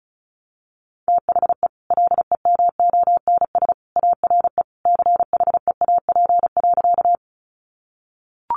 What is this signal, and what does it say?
Morse code, heard as a keyed tone, about 34 wpm, 700 Hz. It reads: THE LEMONS ARE CHEAP.
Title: Additional Easy Phrases 35wpm